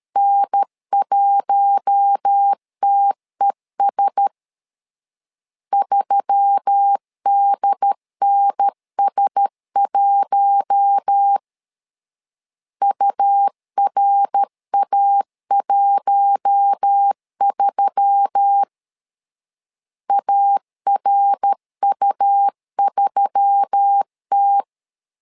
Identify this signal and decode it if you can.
Morse code, heard as a keyed tone, about 13 wpm, 800 Hz. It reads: N1TES 3DNS1 URA13 ARU3T
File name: LEZIONI / 4 Q1S3 12wpm 2023